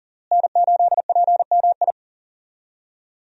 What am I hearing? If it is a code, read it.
Morse code, heard as a keyed tone, about 40 wpm, 700 Hz. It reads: N8PMI
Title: USA Call Signs - 5 Characters - Repeated 3x - 40wpm